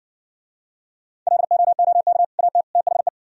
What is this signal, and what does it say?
Morse code, heard as a keyed tone, about 60 wpm, 700 Hz. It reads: LOOK AT THE